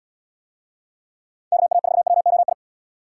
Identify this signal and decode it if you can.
Morse code, heard as a keyed tone, about 70 wpm, 700 Hz. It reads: KA4WOI